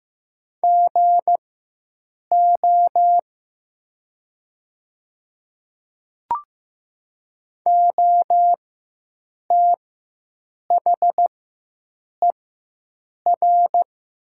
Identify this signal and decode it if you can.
Morse code, heard as a keyed tone, about 15 wpm, spaced as for 7 wpm, 700 Hz. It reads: GO OTHER
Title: 4x Character Spacing - Top 100 Words - 15wpm